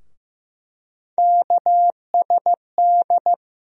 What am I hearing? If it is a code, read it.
Morse code, heard as a keyed tone, about 15 wpm, 700 Hz. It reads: KSD
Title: USA Call Sign Postfixes - 3 Letters 15wpm